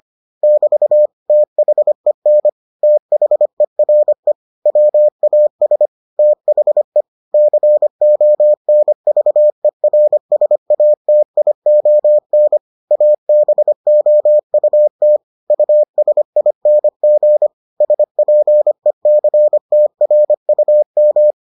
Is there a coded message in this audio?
Yes: Morse code from 0.4 to 21.3 s, heard as a keyed tone, about 25 wpm, 600 Hz. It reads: <BT> THEN THERE WAS THE CONVERSATION ABOUT USING SPECTRUM